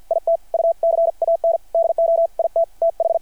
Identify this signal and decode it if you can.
Morse code, heard as a keyed tone, about 53 wpm, 670 Hz. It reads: IT U CAN DO IT TH
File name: archived / CRSnet